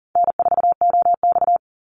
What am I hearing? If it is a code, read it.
Morse code, heard as a keyed tone, about 42 wpm, 700 Hz. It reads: N4OX
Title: Easy Call Signs - 40wpm